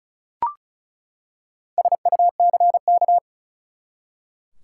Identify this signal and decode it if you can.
Morse code, heard as a keyed tone, about 34 wpm, 700 Hz. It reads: SUCK
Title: Four Letter Words - 35wpm